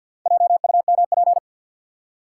Morse code, heard as a keyed tone, about 53 wpm, 700 Hz. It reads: JUMP
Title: Four Letter Words - 50wpm